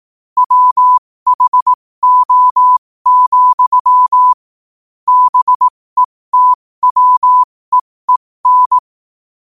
Morse code, heard as a keyed tone, about 18 wpm, spaced as for 15 wpm, 1000 Hz. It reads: WHO, BETWEEN
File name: qotdfiles / 15wpm